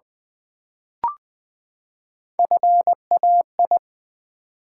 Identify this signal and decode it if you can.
Morse code, heard as a keyed tone, about 20 wpm, 700 Hz. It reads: FAI
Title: Build A Word - 3 Letters At A Time - 6-Letter Words - 20wpm